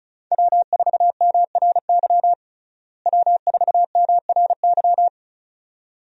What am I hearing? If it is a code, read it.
Morse code, heard as a keyed tone, about 34 wpm, 700 Hz. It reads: W4MRY W4MRY
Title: USA Call Signs - 5 Characters - Repeated 2x - 35wpm